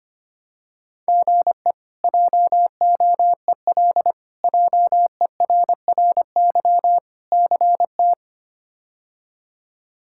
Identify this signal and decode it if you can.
Morse code, heard as a keyed tone, about 25 wpm, 700 Hz. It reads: GE JOEL JERRY CT
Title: K1USN SST QSO Practice - 25wpm